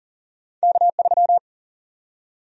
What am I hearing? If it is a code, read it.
Morse code, heard as a keyed tone, about 40 wpm, 700 Hz. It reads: K3
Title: USA Call Sign Prefixes - Repeated 2x - 40wpm